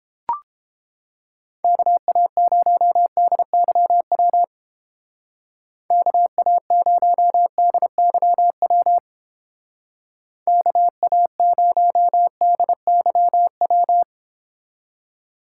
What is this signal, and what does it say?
Morse code, heard as a keyed tone, about 30 wpm, 700 Hz. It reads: KA0DYW KA0DYW KA0DYW